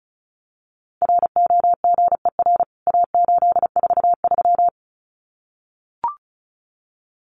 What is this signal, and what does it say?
Morse code, heard as a keyed tone, about 34 wpm, 700 Hz. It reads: ROGER A843
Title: CWT Exchange 35wpm